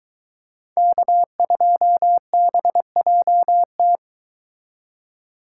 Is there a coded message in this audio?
Yes: Morse code from 0.8 to 4.0 s, heard as a keyed tone, about 22 wpm, 700 Hz. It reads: K2BJT